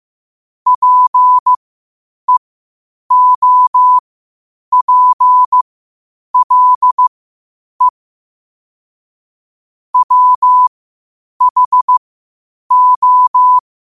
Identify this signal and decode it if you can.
Morse code, heard as a keyed tone, about 15 wpm, spaced as for 8 wpm, 1000 Hz. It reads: PEOPLE WHO
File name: qotdfiles / 8wpm